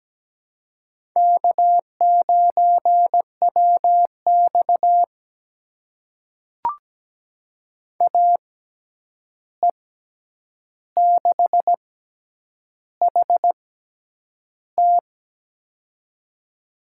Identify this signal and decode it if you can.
Morse code, heard as a keyed tone, about 17 wpm, spaced as for 8 wpm, 700 Hz. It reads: K9WX AE6HT
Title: USA Call Signs - 6x Character Spacing - 17wpm